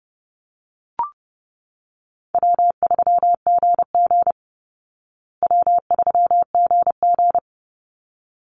Morse code, heard as a keyed tone, about 30 wpm, 700 Hz. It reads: W3GG W3GG
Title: Easy Call Signs Repeated 2x - 30wpm